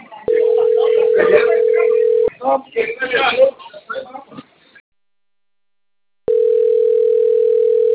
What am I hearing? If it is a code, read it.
Ringback tone (North America), 440 + 480 Hz, with speech.
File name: rg-1600-unknown- - - .63461